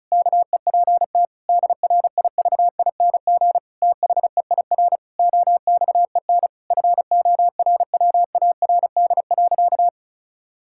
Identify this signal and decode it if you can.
Morse code, heard as a keyed tone, about 34 wpm, 700 Hz. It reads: KEPT DRIVING THEIR OXEN FORWARD.
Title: Pangram Challenge v2 - 35wpm